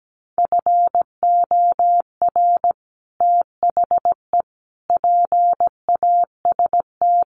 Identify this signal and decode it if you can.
Morse code, heard as a keyed tone, about 17 wpm, 700 Hz. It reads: FOR THE PAST
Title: Sets of 5 Words Encore2 17wpm